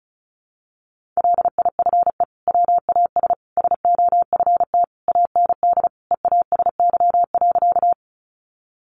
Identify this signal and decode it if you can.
Morse code, heard as a keyed tone, about 34 wpm, 700 Hz. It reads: LIFE WAS SOFT AND EASY.
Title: Sentences from Top 1500 Words 35wpm